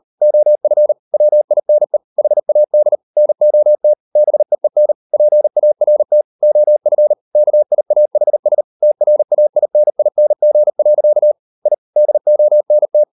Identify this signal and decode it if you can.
Morse code, heard as a keyed tone, about 38 wpm, 600 Hz. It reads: OF WINE HAD NOT BEEN PART OF KIAHS TRAINING. I DONT